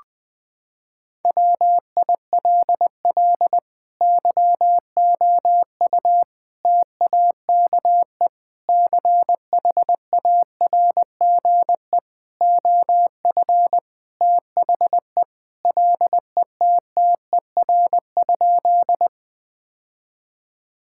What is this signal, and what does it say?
Morse code, heard as a keyed tone, about 20 wpm, 700 Hz. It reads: WILL YOU TAKE CHARGE OF THE LETTER?
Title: Sentences from Top 1000 Words - Review Part 6 - 20wpm